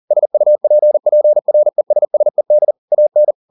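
Morse code, heard as a keyed tone, about 40 wpm, 600 Hz. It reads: SUPPRESSED AN
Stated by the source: Morse code at 40 wpm, practice text, CW med Gnister.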